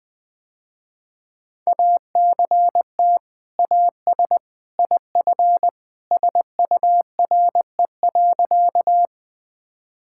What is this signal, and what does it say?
Morse code, heard as a keyed tone, about 20 wpm, 700 Hz. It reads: ACT AS IF SURE.
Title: Sentences with 2-Word Phrases - Spoken Beginning - 20wpm